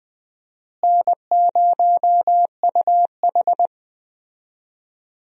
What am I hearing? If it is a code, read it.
Morse code, heard as a keyed tone, about 20 wpm, 700 Hz. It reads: N0UH
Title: USA Call Signs - 4 Characters - Repeated 2x - 20wpm